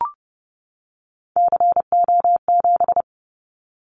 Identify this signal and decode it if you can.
Morse code, heard as a keyed tone, about 30 wpm, 700 Hz. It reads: CO7
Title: Callsign Prefixes for Top 100 DXCC - 30wpm